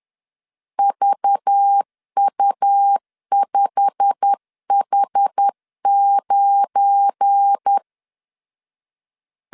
Morse code, heard as a keyed tone, about 10 wpm, 800 Hz. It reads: VU5H9